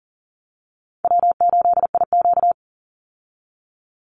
Morse code, heard as a keyed tone, about 40 wpm, 700 Hz. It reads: W8IQ
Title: Call Signs Repeated 2x - 40wpm